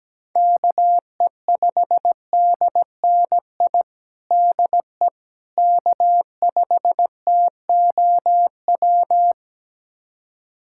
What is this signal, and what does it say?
Morse code, heard as a keyed tone, about 17 wpm, 700 Hz. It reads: KE5DNI DE K5TOW
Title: USA Callsign DE Callsign 17wpm